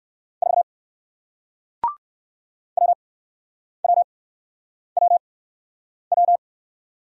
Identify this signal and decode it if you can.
Morse code, heard as a keyed tone, about 61 wpm, spaced as for 9 wpm, 700 Hz. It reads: 4 WWWW